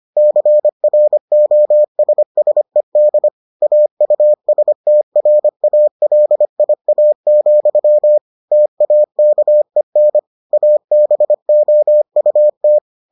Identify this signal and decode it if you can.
Morse code, heard as a keyed tone, about 25 wpm, 600 Hz. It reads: CROSSED AUSTRALIA, TAKEN ABOUT